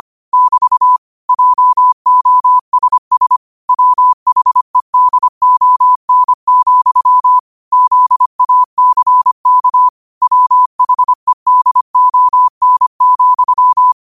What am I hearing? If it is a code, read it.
Morse code, heard as a keyed tone, about 25 wpm, 1000 Hz. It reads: <BT> JOSS WHEDON, ZACK WHEDON,
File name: qotdfiles / 25wpm